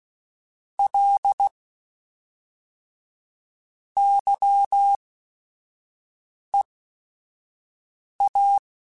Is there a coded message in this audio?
Yes: Morse code from 0.8 to 8.6 s, heard as a keyed tone, about 16 wpm, spaced as for 4 wpm, 800 Hz. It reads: LY EA